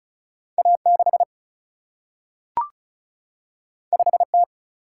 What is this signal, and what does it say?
Morse code, heard as a keyed tone, about 36 wpm, 700 Hz. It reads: A 6 5 T
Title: Single-Letter Single-Number 35wpm